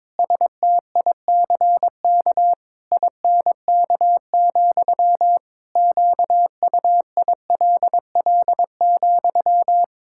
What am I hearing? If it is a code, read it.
Morse code, heard as a keyed tone, about 22 wpm, 700 Hz. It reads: STICK INK, QUILL,
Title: Pangram Challenge v2 - 22wpm